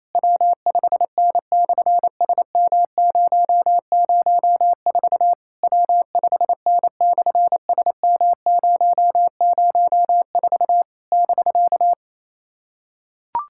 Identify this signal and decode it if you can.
Morse code, heard as a keyed tone, about 28 wpm, 700 Hz. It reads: W5N/SM004 W5N/SM004 <BK>
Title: SOTA QSO Practice 28wpm